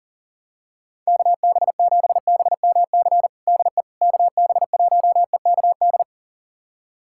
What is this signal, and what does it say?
Morse code, heard as a keyed tone, about 40 wpm, 700 Hz. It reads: KB7BMC DE KB1EKD
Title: USA Callsign DE Callsign - 6 Characters - 40wpm